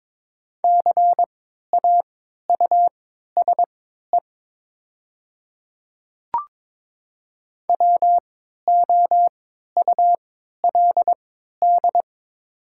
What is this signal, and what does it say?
Morse code, heard as a keyed tone, about 22 wpm, spaced as for 13 wpm, 700 Hz. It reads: CAUSE WOULD
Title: 3x Character Spacing - Top 500 Words - 22wpm